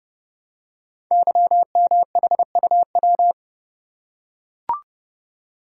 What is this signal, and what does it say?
Morse code, heard as a keyed tone, about 30 wpm, 700 Hz. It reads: YMHUW
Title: Code Groups - 5 Letter:Numbers - 30wpm